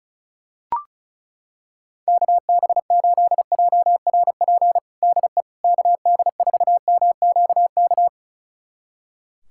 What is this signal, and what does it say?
Morse code, heard as a keyed tone, about 36 wpm, 700 Hz. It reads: KB8JRP DE KD4MQK
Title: USA Callsign DE Callsign - 6 Characters - 35wpm